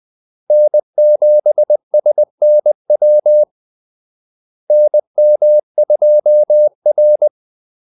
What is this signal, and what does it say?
Morse code, heard as a keyed tone, about 20 wpm, 600 Hz. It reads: N7SNW NM2R
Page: Kaldesignaler 20 wpm